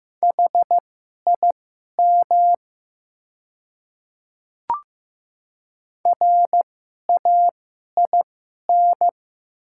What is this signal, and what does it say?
Morse code, heard as a keyed tone, about 15 wpm, spaced as for 11 wpm, 700 Hz. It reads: HIM RAIN
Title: 2x Character Spacing - Top 300 Words - 15wpm